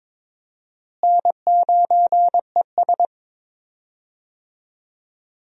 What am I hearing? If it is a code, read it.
Morse code, heard as a keyed tone, about 22 wpm, 700 Hz. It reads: N9ES
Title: USA Call Signs - 4 Characters - Repeated 3x - 22wpm